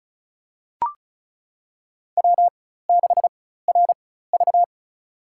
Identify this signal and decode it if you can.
Morse code, heard as a keyed tone, about 36 wpm, spaced as for 16 wpm, 700 Hz. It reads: W6RV
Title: USA Call Signs - 4 Characters - 4x Character Spacing - 35wpm